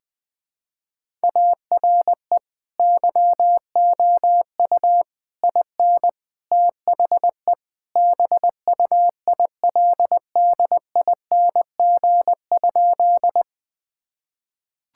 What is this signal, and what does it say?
Morse code, heard as a keyed tone, about 20 wpm, 700 Hz. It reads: ARE YOU IN THE BUILDING?
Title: Sentences from Top 1000 Words - Review Part 5 - 20wpm